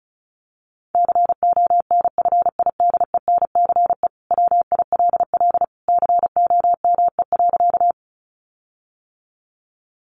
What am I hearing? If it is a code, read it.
Morse code, heard as a keyed tone, about 34 wpm, 700 Hz. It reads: CONFIDENCE WILL COME.
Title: Sentences from Top 1500 Words 35wpm